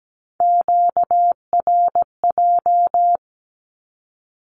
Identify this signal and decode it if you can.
Morse code, heard as a keyed tone, about 17 wpm, 700 Hz. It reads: QRJ
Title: Q-Codes with Spoken Meaning - 17wpm